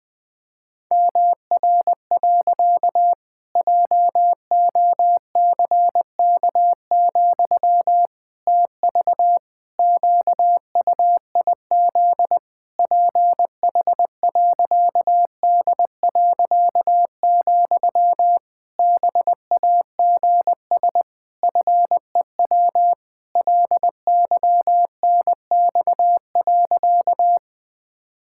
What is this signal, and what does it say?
Morse code, heard as a keyed tone, about 20 wpm, 700 Hz. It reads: MR. JOCK, TV QUIZ PH.D., BAGS FEW LYNX.
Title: Pangram Challenge v2 - 20wpm